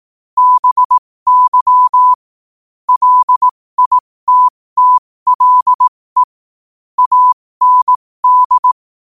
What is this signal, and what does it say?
Morse code, heard as a keyed tone, about 18 wpm, spaced as for 15 wpm, 1000 Hz. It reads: BY LITTLE AND
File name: qotdfiles / 15wpm